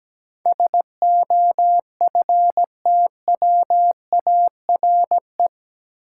Morse code, heard as a keyed tone, about 17 wpm, 700 Hz. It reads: SOFTWARE
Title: Sentences from Top 1000 Words - Review Part 3 - New Word Drill - 17wpm